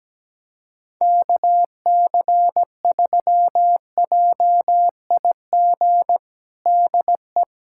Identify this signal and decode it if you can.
Morse code, heard as a keyed tone, about 17 wpm, 700 Hz. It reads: KC3JIG DE
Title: USA Callsign DE Callsign 17wpm